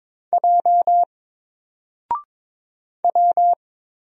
Morse code, heard as a keyed tone, about 23 wpm, 700 Hz. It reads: J W
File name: Single-Letter-Number-22wpm